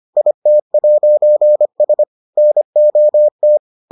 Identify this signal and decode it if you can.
Morse code, heard as a keyed tone, about 25 wpm, 600 Hz. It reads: IT'S NOT